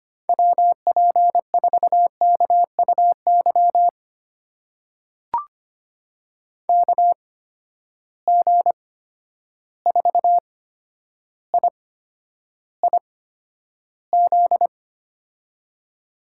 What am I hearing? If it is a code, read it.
Morse code, heard as a keyed tone, about 25 wpm, spaced as for 11 wpm, 700 Hz. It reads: WP4KUY KG4IIZ
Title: USA Call Signs - 6 Characters - 8x Character Spacing - 25wpm